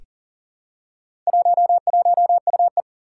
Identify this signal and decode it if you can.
Morse code, heard as a keyed tone, about 38 wpm, 700 Hz. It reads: 11UE
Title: Four Challenging Letter-Number Combinations 40wpm